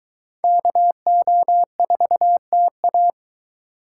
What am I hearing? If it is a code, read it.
Morse code, heard as a keyed tone, about 22 wpm, 700 Hz. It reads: KO4TA